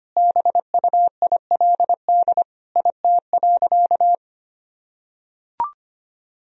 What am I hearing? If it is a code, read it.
Morse code, heard as a keyed tone, about 25 wpm, 700 Hz. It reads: BUILD IT.
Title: Sentences from Top 1000 Words - Review Part 1 - 25wpm